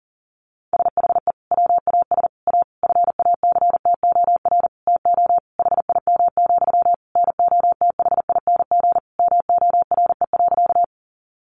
Morse code, heard as a keyed tone, about 40 wpm, 700 Hz. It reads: SHE WAS A FACTOR TO HIM, NOTHING MORE.
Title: Sentences from Top 1300 Words 40wpm